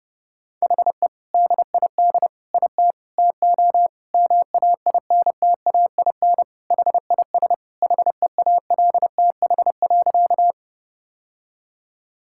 Morse code, heard as a keyed tone, about 32 wpm, 700 Hz. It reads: HE DID IT TO MAINTAIN HIS HEALTH.
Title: Sentences from Top 1000 Words - Review Part 5 - 30wpm